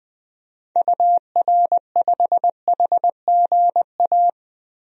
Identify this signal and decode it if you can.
Morse code, heard as a keyed tone, about 20 wpm, 700 Hz. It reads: UR5HGA
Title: DXCC-100 Call Signs - 6 Characters - 20wpm